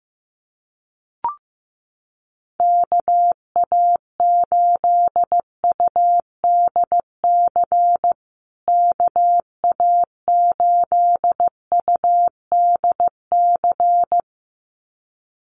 Morse code, heard as a keyed tone, about 15 wpm, 700 Hz. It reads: KA8UDC KA8UDC
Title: USA Call Signs Repeated 2x 15wpm